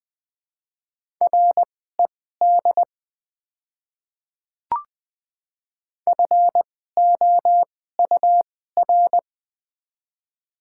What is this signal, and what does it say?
Morse code, heard as a keyed tone, about 20 wpm, spaced as for 14 wpm, 700 Hz. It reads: RED FOUR
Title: 2x Character Spacing - Top 500 Words - 20wpm